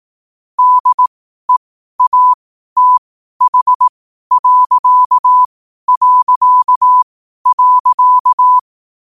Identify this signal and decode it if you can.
Morse code, heard as a keyed tone, about 18 wpm, spaced as for 12 wpm, 1000 Hz. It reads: DEATH...
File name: qotdfiles / 12wpm